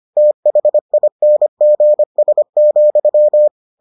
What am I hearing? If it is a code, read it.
Morse code, heard as a keyed tone, about 25 wpm, 600 Hz. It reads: THINGS,